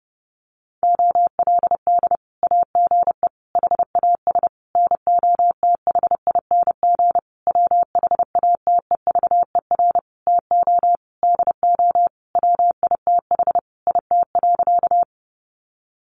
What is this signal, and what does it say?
Morse code, heard as a keyed tone, about 30 wpm, 700 Hz. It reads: OLD AGE HAS NOTHING WHATEVER TO DO WITH IT.
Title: Sentences from Top 1100 Words 30wpm